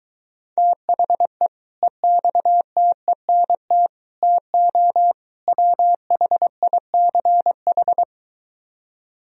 Morse code, heard as a keyed tone, about 23 wpm, 700 Hz. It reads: THE EXTENT TO WHICH